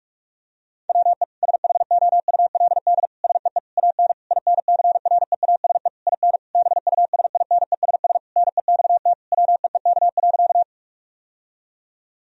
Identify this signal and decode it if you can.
Morse code, heard as a keyed tone, about 41 wpm, 700 Hz. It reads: WE SHOULD SEE AN INCREASE IN BUSINESS NEXT WEEK.
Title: Sentences from Top 1000 Words - Review Part 4 - 45wpm